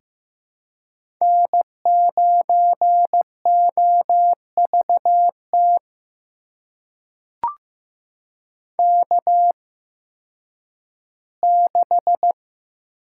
Morse code, heard as a keyed tone, about 15 wpm, spaced as for 10 wpm, 700 Hz. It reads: N9OVT K6
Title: USA Call Signs - 5 Characters - 8x Character Spacing - 15wpm